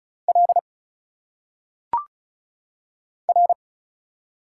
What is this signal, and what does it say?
Morse code, heard as a keyed tone, about 34 wpm, 700 Hz. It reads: L R
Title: Single Letters - Phonetic Pronunciation 35wpm